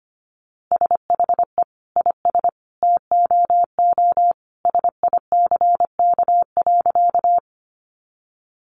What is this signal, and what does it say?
Morse code, heard as a keyed tone, about 25 wpm, 700 Hz. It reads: SHE IS TOO SICK.
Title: Easy Phrases 25wpm